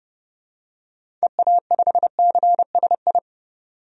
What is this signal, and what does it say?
Morse code, heard as a keyed tone, about 31 wpm, 700 Hz. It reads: EA5CSI
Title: DXCC-100 Call Signs - 6 Characters - 30wpm